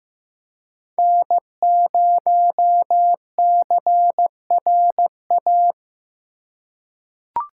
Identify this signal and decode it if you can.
Morse code, heard as a keyed tone, about 15 wpm, 700 Hz. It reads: N0CRA
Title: USA Call Signs - 2x Character Spacing - 15wpm